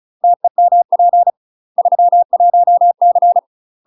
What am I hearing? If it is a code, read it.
Morse code, heard as a keyed tone, about 36 wpm, 700 Hz. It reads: TEMP 31C